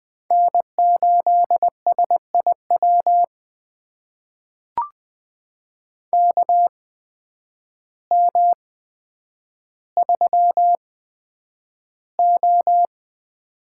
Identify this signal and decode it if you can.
Morse code, heard as a keyed tone, about 20 wpm, spaced as for 9 wpm, 700 Hz. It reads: N8SIW KM3O
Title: USA Call Signs - 8x Character Spacing - 20wpm